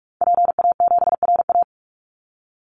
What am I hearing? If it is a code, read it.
Morse code, heard as a keyed tone, about 44 wpm, 700 Hz. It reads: PA7RA
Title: Call Signs Repeated 3x - 45wpm